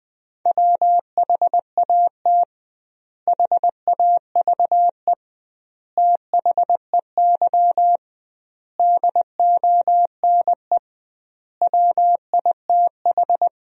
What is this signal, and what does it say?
Morse code, heard as a keyed tone, about 20 wpm, 700 Hz. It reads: WHAT HAVE THEY DONE WITH
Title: 2x Word-Spacing - Sentences from Top 500 Words - 20wpm